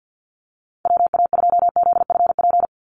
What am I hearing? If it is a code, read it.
Morse code, heard as a keyed tone, about 50 wpm, 700 Hz. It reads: WA2ZFP
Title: USA Call Signs 50wpm